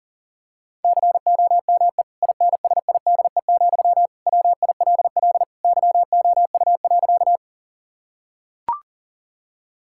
Morse code, heard as a keyed tone, about 40 wpm, 700 Hz. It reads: COME INSIDE, WILL YOU.
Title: Sentences from Top 1000 Words - Review Part 6 - 40wpm